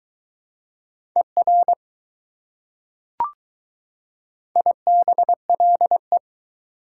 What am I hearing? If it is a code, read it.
Morse code, heard as a keyed tone, about 20 wpm, 700 Hz. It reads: ER IBLE